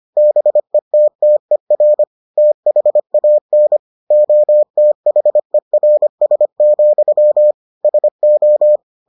Morse code, heard as a keyed tone, about 25 wpm, 600 Hz. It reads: BETTER THAN OTHERS, SO